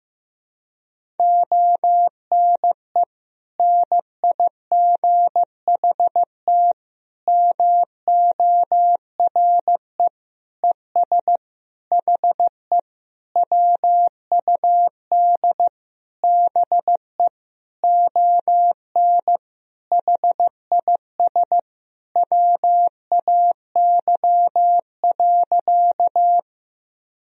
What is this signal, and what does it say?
Morse code, heard as a keyed tone, about 15 wpm, 700 Hz. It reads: ONE NIGHT MORE ES HE WUD BE ON HIS WAY.
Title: Sentences from Top 300 Words and Abbreviations 15wpm